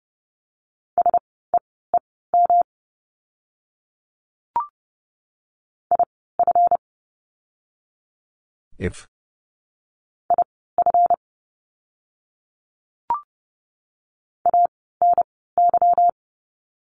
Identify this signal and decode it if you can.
Morse code, heard as a keyed tone, about 30 wpm, spaced as for 17 wpm, 700 Hz. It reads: SEEM IF IF ANY
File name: course / 30:10wpm / 237 Lesson 38f - Letter q - Words 30wpm